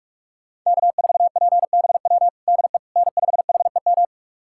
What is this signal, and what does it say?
Morse code, heard as a keyed tone, about 46 wpm, 700 Hz. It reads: K4PBW DE N5HEM
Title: USA Callsign DE Callsign - 5 Characters - 45wpm